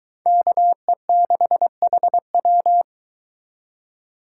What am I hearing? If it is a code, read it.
Morse code, heard as a keyed tone, about 22 wpm, 700 Hz. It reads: KE6HW